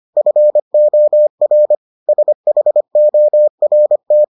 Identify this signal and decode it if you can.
Morse code, heard as a keyed tone, about 25 wpm, 600 Hz. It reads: FOR SHORT